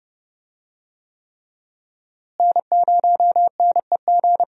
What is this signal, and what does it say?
Morse code, heard as a keyed tone, about 30 wpm, 700 Hz. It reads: N0NEG
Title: USA Call Signs - 5 Characters - Repeated 3x - 30wpm